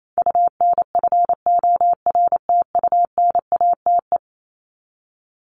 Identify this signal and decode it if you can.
Morse code, heard as a keyed tone, about 28 wpm, 700 Hz. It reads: UNFORTUNATE
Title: LY Suffix - Top 100 Words with Base Words 28wpm